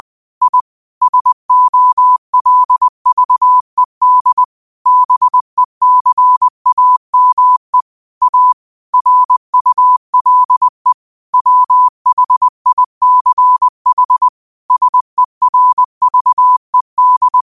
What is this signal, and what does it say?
Morse code, heard as a keyed tone, about 20 wpm, 1000 Hz. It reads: I SOLVED BECAME A RULE WHICH SERVED
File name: qotdfiles / 20wpm